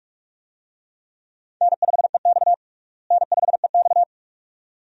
Morse code, heard as a keyed tone, about 44 wpm, 700 Hz. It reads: N5EX N5EX
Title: USA Call Signs - 4 Characters - Repeated 2x - 45wpm